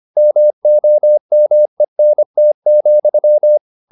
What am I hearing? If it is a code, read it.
Morse code, heard as a keyed tone, about 25 wpm, 600 Hz. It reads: MOMENT,